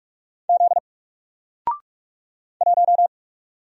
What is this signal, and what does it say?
Morse code, heard as a keyed tone, about 46 wpm, 700 Hz. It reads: Z 1
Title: Single Character (Letter:Number:Punctuation:Prosign) 45wpm